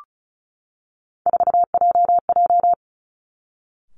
Morse code, heard as a keyed tone, about 36 wpm, 700 Hz. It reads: 4JJ
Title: Three Challenging Letter-Number Combinations 35wpm